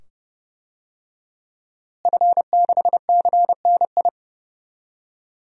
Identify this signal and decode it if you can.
Morse code, heard as a keyed tone, about 30 wpm, 700 Hz. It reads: F6CNI
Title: DXCC-100 Call Signs - 5 Characters - Repeated 2x - 30wpm